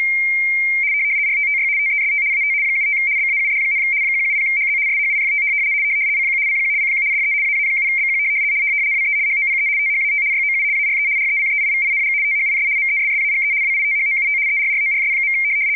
RTTY, about 75 baud. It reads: RTTY TEST AT 75 BAUD, 170HZ SHIFT CQ CQ CQ DE AA7BQ WE HOPE THAT YOU ARE RECEIVING THIS MESSAGE WITHOUT ERRORS.
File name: w ry170 / rtty75.wav / rtty75